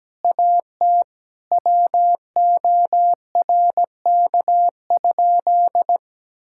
Morse code, heard as a keyed tone, about 17 wpm, 700 Hz. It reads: AT WORK?
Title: Sentences from Top 1000 Words - Review Part 6 - 17wpm